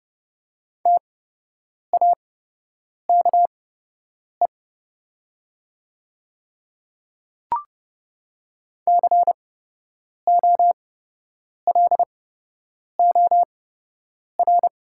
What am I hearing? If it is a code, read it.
Morse code, heard as a keyed tone, about 29 wpm, spaced as for 8 wpm, 700 Hz. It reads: TAKE COLOR
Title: 8x Character Spacing - Top 300 Words - 30wpm